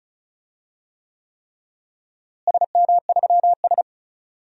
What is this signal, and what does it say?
Morse code, heard as a keyed tone, about 34 wpm, 700 Hz. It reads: SM3S
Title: DXCC-100 Call Signs - 4 Characters - Repeated 3x - 35wpm